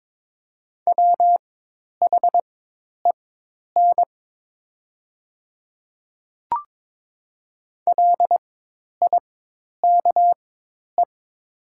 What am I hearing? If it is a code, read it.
Morse code, heard as a keyed tone, about 22 wpm, spaced as for 10 wpm, 700 Hz. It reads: WHEN LIKE
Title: 4x Character Spacing - Top 100 Words - 22wpm